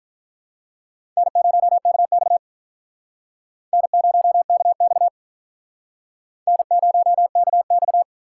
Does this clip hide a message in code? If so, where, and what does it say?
Yes: Morse code from 1.2 to 8.0 s, heard as a keyed tone, about 46 wpm, 700 Hz. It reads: N0KX N0KX N0KX